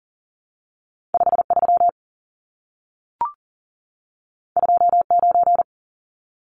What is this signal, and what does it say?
Morse code, heard as a keyed tone, about 40 wpm, 700 Hz. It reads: 53 29
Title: Numbers - 2-Digit 40wpm